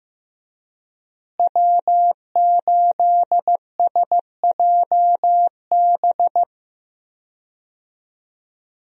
Morse code, heard as a keyed tone, about 15 wpm, 700 Hz. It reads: W8SJB
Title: USA Call Signs - 5 Characters - Repeated 3x - 15wpm